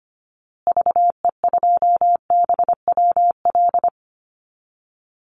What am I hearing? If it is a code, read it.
Morse code, heard as a keyed tone, about 25 wpm, 700 Hz. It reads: VE2BWL
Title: Call Signs Repeated 3x - 25wpm